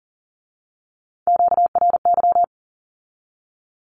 Morse code, heard as a keyed tone, about 38 wpm, 700 Hz. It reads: QRY
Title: Q-Codes with Spoken Meaning - 40wpm